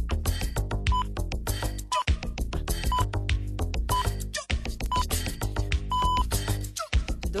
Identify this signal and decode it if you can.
Radio time signal (time pips). It is heard from 0.9 to 6.2 s.